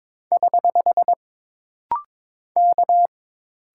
Morse code, heard as a keyed tone, about 22 wpm, 700 Hz. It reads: <HH> K
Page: This rapid-fire timing is geared to push your ICR (Instant Character Recognition) skills.